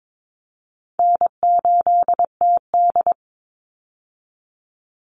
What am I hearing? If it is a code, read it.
Morse code, heard as a keyed tone, about 22 wpm, 700 Hz. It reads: N8TD
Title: Easy Call Signs Repeated 2x - 22wpm